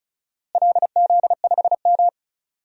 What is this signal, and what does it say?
Morse code, heard as a keyed tone, about 34 wpm, 700 Hz. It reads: LZ5M
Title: DXCC-100 Call Signs - 4 Characters - Repeated 3x - 35wpm